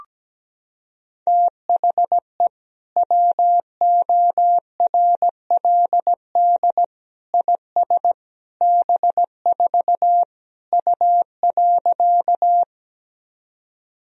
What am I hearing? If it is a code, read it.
Morse code, heard as a keyed tone, about 17 wpm, 700 Hz. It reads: THE WORLD IS B4 U.
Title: Sentences from Top 200 Words and Abbreviations 17wpm